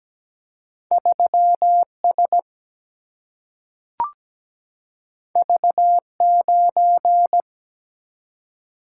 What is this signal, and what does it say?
Morse code, heard as a keyed tone, about 17 wpm, 700 Hz. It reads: 3S V9
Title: Single-Letter Single-Number 17wpm